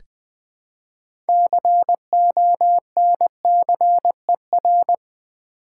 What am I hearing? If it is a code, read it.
Morse code, heard as a keyed tone, about 21 wpm, 700 Hz. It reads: CONCER
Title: Build A Word - 3 Letters At A Time - 9-Letter Words - 20wpm